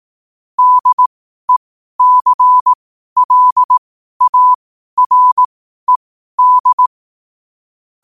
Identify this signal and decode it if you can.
Morse code, heard as a keyed tone, about 18 wpm, spaced as for 12 wpm, 1000 Hz. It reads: DECLARED
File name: qotdfiles / 12wpm